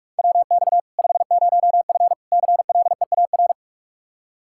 Morse code, heard as a keyed tone, about 41 wpm, 700 Hz. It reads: WX 50F CLEAR
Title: Practice QSO - Exchange 2 - Info - 45wpm